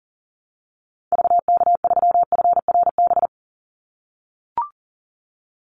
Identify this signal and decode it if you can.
Morse code, heard as a keyed tone, about 42 wpm, 700 Hz. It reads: VK3FRB
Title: USA Call Signs 40wpm